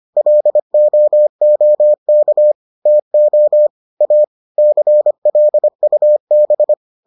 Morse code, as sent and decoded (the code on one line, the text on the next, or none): .-.. --- --- -.- / - --- / .- / -.-. .-.. ..- -...
LOOK TO A CLUB